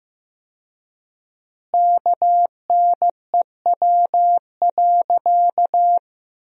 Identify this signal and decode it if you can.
Morse code, heard as a keyed tone, about 16 wpm, 700 Hz. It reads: KNEW.
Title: 4x Word-Spacing - Sentences from Top 400 Words - 15wpm